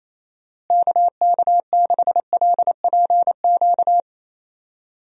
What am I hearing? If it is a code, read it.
Morse code, heard as a keyed tone, about 28 wpm, 700 Hz. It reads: KK6LPQ
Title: USA Call Signs - 6 Characters - Repeated 2x - 28wpm